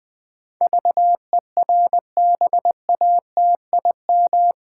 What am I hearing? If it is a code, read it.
Morse code, heard as a keyed tone, about 21 wpm, 700 Hz. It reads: VERBATIM
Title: Pangram Challenge v2 - 20wpm